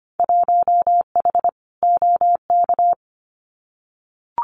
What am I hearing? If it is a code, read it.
Morse code, heard as a keyed tone, about 25 wpm, 700 Hz. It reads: 1H OK
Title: Winter Field Day Exchanges 25wpm